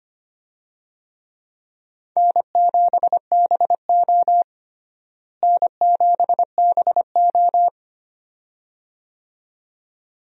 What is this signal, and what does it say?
Morse code, heard as a keyed tone, about 25 wpm, 700 Hz. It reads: N7BO N7BO
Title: USA Call Signs - 4 Characters - Repeated 2x - 25wpm